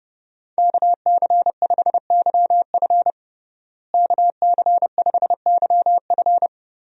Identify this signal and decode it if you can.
Morse code, heard as a keyed tone, about 30 wpm, 700 Hz. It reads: KC5YF KC5YF
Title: USA Call Signs - 5 Characters - Repeated 2x - 30wpm